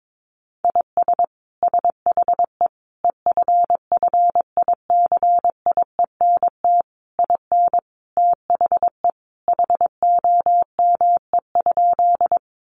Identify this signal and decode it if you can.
Morse code, heard as a keyed tone, about 22 wpm, 700 Hz. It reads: IS SHE EFFICIENT IN THE HOME?
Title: Sentences from Top 1300 Words 22wpm